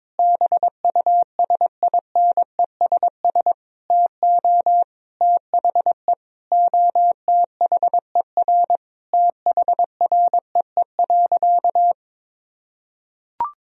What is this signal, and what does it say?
Morse code, heard as a keyed tone, about 22 wpm, 700 Hz. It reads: BUSINESS TO THE OTHER THREE.
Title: Sentences from Top 1000 Words - Review Part 4 - 22wpm